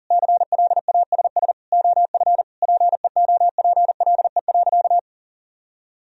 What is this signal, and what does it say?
Morse code, heard as a keyed tone, about 37 wpm, 700 Hz. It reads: CLASS OF PEOPLE.
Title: Sentences from Top 1000 Words - Review Part 2 - 40wpm